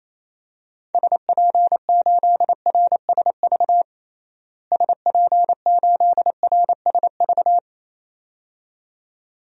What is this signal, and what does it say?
Morse code, heard as a keyed tone, about 27 wpm, 700 Hz. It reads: SP8RSV SP8RSV
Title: DXCC-100 Call Signs - 6 Characters - Repeated 2x - 28wpm